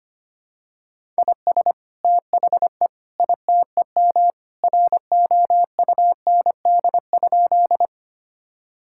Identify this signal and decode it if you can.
Morse code, heard as a keyed tone, about 25 wpm, 700 Hz. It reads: IS THE ITEM ROUND?
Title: Sentences from Top 1000 Words - Review Part 6 - 25wpm